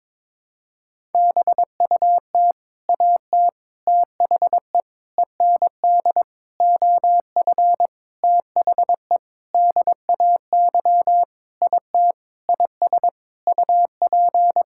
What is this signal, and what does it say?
Morse code, heard as a keyed tone, about 22 wpm, 700 Hz. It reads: BUT AT THE END OF THE DAY IT IS UP
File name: Sentences with 7-Word Phrases - 22wpm